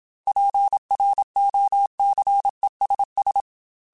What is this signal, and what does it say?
Morse code, heard as a keyed tone, about 27 wpm, 800 Hz. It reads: PROCESS
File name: Technical Subjects / Technical Subjects-mp3s / AND Gate @25